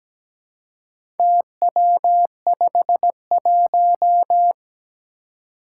Morse code, heard as a keyed tone, about 17 wpm, 700 Hz. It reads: TW51
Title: Four Challenging Letter-Number Combinations 17wpm